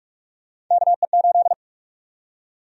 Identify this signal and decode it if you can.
Morse code, heard as a keyed tone, about 46 wpm, 700 Hz. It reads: KE8
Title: USA Call Sign Prefixes - 45wpm